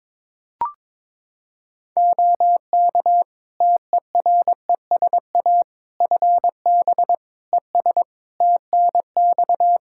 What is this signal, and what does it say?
Morse code, heard as a keyed tone, about 22 wpm, 700 Hz. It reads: OK TERESA FB ES TNX
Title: Practice QSO - Exchange 2 - Info - 22wpm